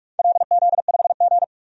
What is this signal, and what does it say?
Morse code, heard as a keyed tone, about 46 wpm, 700 Hz. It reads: LZ5G
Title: DXCC-100 Call Signs - 4 Characters - 45wpm